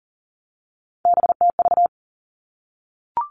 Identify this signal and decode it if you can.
Morse code, heard as a keyed tone, about 39 wpm, 700 Hz. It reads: BTV
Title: Three Challenging Letter-Number Combinations 40wpm